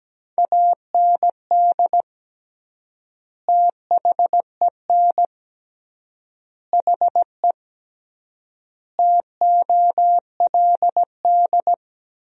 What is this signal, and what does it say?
Morse code, heard as a keyed tone, about 17 wpm, 700 Hz. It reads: AND THEN HE TOLD
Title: 3x Word-Spacing - Sentences from Top 400 Words - 17wpm